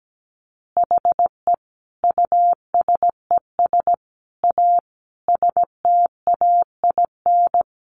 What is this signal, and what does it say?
Morse code, heard as a keyed tone, about 17 wpm, 700 Hz. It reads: HE USES A STAIN
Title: Additional Easy Phrases 17wpm